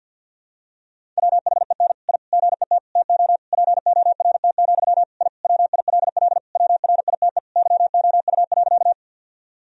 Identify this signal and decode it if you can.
Morse code, heard as a keyed tone, about 52 wpm, 700 Hz. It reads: WHEN I GET TO PORT, I WILL WRITE YOU.
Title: Sentences from Top 1000 Words - Review Part 1 - 50wpm